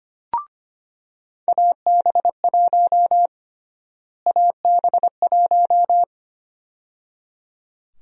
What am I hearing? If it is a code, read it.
Morse code, heard as a keyed tone, about 25 wpm, 700 Hz. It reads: AB1 AB1
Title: USA Call Sign Prefixes - Repeated 2x - 25wpm